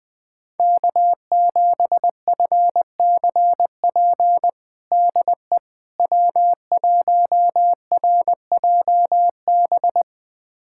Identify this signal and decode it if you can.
Morse code, heard as a keyed tone, about 20 wpm, 700 Hz. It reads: K7FCP DE W1RJB
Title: USA Callsign DE Callsign - 5 Characters - 20wpm